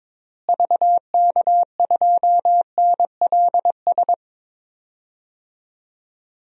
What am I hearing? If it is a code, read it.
Morse code, heard as a keyed tone, about 21 wpm, 700 Hz. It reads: VK2NLS
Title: DXCC-100 Call Signs - 6 Characters - Repeated 3x - 22wpm